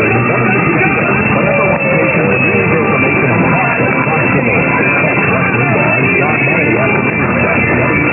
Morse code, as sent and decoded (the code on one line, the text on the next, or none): none